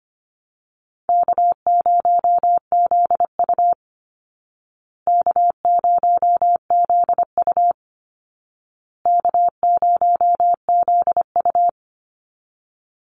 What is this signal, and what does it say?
Morse code, heard as a keyed tone, about 25 wpm, 700 Hz. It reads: K0ZU K0ZU K0ZU
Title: Easy Call Signs Repeated 3x - 25wpm